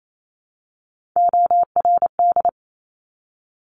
Morse code, heard as a keyed tone, about 28 wpm, 700 Hz. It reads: ORD
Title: Top Three Letter Combinations in Words 28wpm